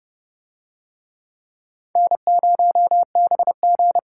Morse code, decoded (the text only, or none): N0BG